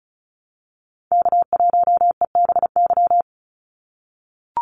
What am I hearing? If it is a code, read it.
Morse code, heard as a keyed tone, about 34 wpm, 700 Hz. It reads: K1EBY
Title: Call Signs - 35wpm